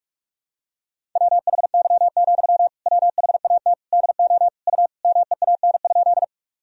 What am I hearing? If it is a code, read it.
Morse code, heard as a keyed tone, about 44 wpm, 700 Hz. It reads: WHY, WHAT DO U MEAN?
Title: Sentences from Top 200 Words and Abbreviations 45wpm